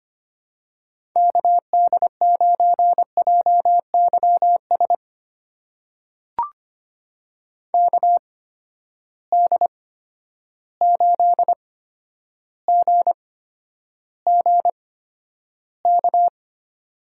Morse code, heard as a keyed tone, about 25 wpm, spaced as for 11 wpm, 700 Hz. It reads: KD9JYS KD8GGK
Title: USA Call Signs - 6 Characters - 8x Character Spacing - 25wpm